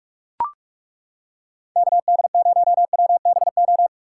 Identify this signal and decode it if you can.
Morse code, heard as a keyed tone, about 34 wpm, 700 Hz. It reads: KD0WBO
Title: Speed Racing - USA Call Signs 30wpm